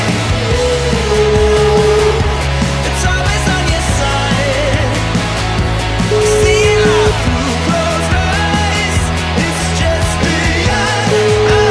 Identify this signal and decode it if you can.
Ringback tone (Europe, 425 Hz), with noise.